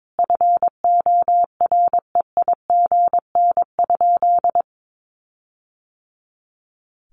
Morse code, heard as a keyed tone, about 22 wpm, 700 Hz. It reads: FOREIGN?
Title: Sentences from Top 1400 Words 22wpm